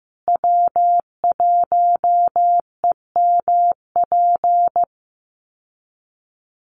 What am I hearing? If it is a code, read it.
Morse code, heard as a keyed tone, about 15 wpm, 700 Hz. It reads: W1EMP
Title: USA Call Signs Repeated 3x 15wpm